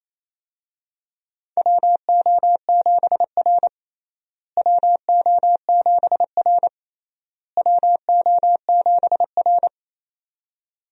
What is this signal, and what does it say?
Morse code, heard as a keyed tone, about 27 wpm, 700 Hz. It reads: WO7R WO7R WO7R
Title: USA Call Signs - 4 Characters - Repeated 3x - 28wpm